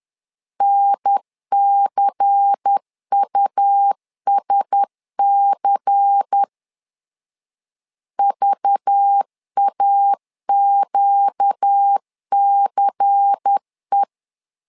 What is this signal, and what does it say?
Morse code, heard as a keyed tone, about 10 wpm, 800 Hz. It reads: NCUSC VAQCE
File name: LEZIONI / 5 LBVC / 5 LBVC 10wpm 2349